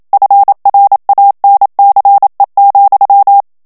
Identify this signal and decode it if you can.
Morse code, heard as a keyed tone, about 27 wpm, 800 Hz. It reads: FRANCE,
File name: CW / FICHIERS-MP3 / FRANCAIS TEXTE CLAIR 1 VITESSE 26 MOTS MINUTES